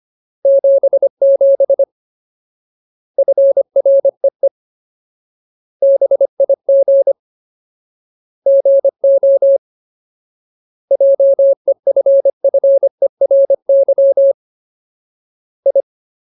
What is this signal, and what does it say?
Morse code, heard as a keyed tone, about 25 wpm, 550 Hz. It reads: 77 FREE BIG GO JEFFERY I
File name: Ultimate Word List / 25WPM Ultimate